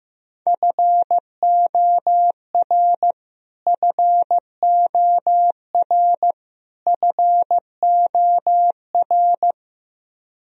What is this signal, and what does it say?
Morse code, heard as a keyed tone, about 15 wpm, 700 Hz. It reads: FOR FOR FOR
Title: Top 100 Words Repeated 15wpm